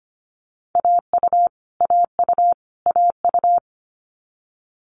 Morse code, heard as a keyed tone, about 25 wpm, 700 Hz. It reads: AU AU AU
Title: Multi-letter Phonemes Repeated 3x 25wpm